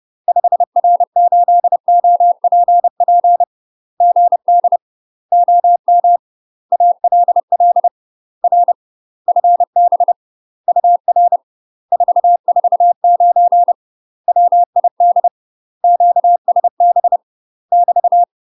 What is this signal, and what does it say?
Morse code, heard as a keyed tone, about 30 wpm, 700 Hz. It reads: 5R8OPP GD OM ALL R FB UR 449 WID QSB <BT>
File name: UK MASTER QSOs TRAINING / 30WPM / UK MASTER QSOs 0008